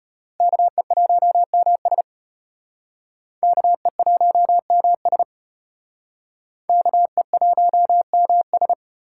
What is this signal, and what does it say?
Morse code, heard as a keyed tone, about 33 wpm, 700 Hz. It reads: KE1MS KE1MS KE1MS